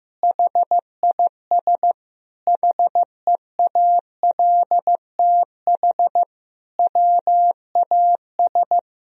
Morse code, heard as a keyed tone, about 16 wpm, 700 Hz. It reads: HIS HEALTH WAS
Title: Sentences from Top 1000 Words - Review Part 3 - 15wpm